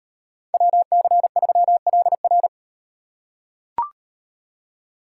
Morse code, heard as a keyed tone, about 36 wpm, 700 Hz. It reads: WC3LR